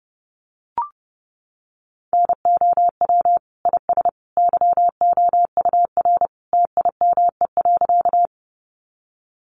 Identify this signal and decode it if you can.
Morse code, heard as a keyed tone, about 30 wpm, 700 Hz. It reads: NOW IS YOUR TIME.
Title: Sentences from Top 100 Words 30wpm